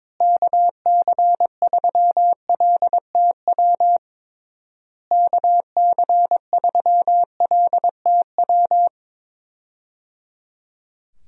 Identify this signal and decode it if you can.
Morse code, heard as a keyed tone, about 22 wpm, 700 Hz. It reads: KC3LTW KC3LTW
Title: USA Call Signs - 6 Characters - Repeated 3x - 22wpm